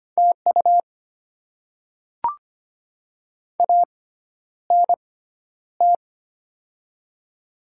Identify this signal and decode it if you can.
Morse code, heard as a keyed tone, about 25 wpm, spaced as for 11 wpm, 700 Hz. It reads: TU ANT
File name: Top 20 QSO Elements - 6x Character Spacing - 25wpm